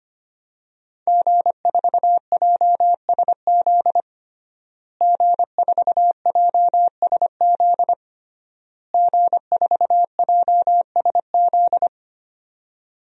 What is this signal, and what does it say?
Morse code, heard as a keyed tone, about 25 wpm, 700 Hz. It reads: G4JSZ G4JSZ G4JSZ
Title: DXCC-100 Call Signs - 5 Characters - Repeated 3x - 25wpm